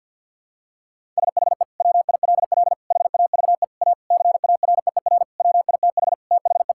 Morse code, heard as a keyed tone, about 50 wpm, 700 Hz. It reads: SHE WILL HAVE A CAREER WITH THE
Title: Sentences from Top 1000 Words - Review Part 6 - 50wpm